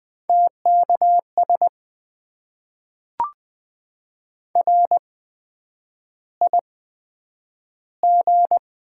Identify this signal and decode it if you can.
Morse code, heard as a keyed tone, about 20 wpm, spaced as for 9 wpm, 700 Hz. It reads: TKS RIG
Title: Top 90 QSO Elements - 8x Character Spacing - 20wpm